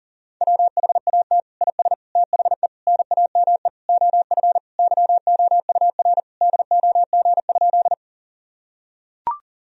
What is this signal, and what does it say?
Morse code, heard as a keyed tone, about 40 wpm, 700 Hz. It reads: WHAT IS THE NAME OF YOUR DOG?
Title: Sentences from Top 1000 Words - Review Part 2 - 40wpm